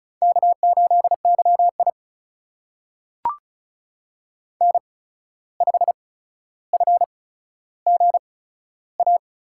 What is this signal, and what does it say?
Morse code, heard as a keyed tone, about 36 wpm, spaced as for 14 wpm, 700 Hz. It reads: K8YI N5FGA
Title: USA Call Signs - 8x Character Spacing - 35wpm